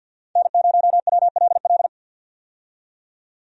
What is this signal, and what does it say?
Morse code, heard as a keyed tone, about 50 wpm, 700 Hz. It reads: N0WLL
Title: USA Call Signs - 5 Characters - 50wpm